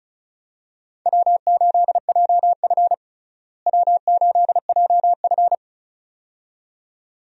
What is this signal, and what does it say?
Morse code, heard as a keyed tone, about 34 wpm, 700 Hz. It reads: W8JF W8JF
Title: USA Call Signs - 4 Characters - Repeated 2x - 35wpm